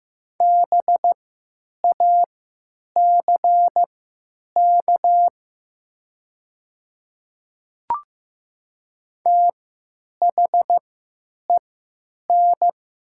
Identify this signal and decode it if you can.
Morse code, heard as a keyed tone, about 15 wpm, spaced as for 9 wpm, 700 Hz. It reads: BACK THEN
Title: 3x Character Spacing - Top 200 Words - 15wpm